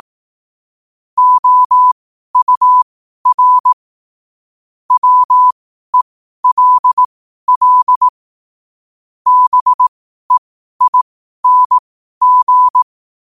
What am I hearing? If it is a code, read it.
Morse code, heard as a keyed tone, about 18 wpm, spaced as for 12 wpm, 1000 Hz. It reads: OUR WELL BEING
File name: qotdfiles / 12wpm